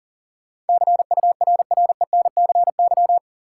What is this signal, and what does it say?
Morse code, heard as a keyed tone, about 40 wpm, 700 Hz. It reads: CURRENCY
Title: Sentences from Top 1500 Words - New Word Drill - 40wpm